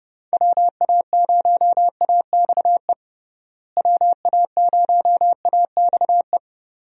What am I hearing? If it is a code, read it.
Morse code, heard as a keyed tone, about 30 wpm, 700 Hz. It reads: WA0AXE WA0AXE
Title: USA Call Signs - 6 Characters - Repeated 2x - 30wpm